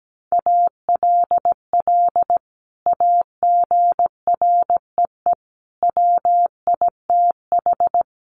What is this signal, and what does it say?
Morse code, heard as a keyed tone, about 17 wpm, 700 Hz. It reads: ALL AGREE WITH
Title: Sentences from Top 1100 Words 17wpm